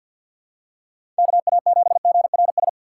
Morse code, heard as a keyed tone, about 47 wpm, 700 Hz. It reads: KA7GRS
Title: USA Call Signs - 6 Characters - 6x Character Spacing - 50wpm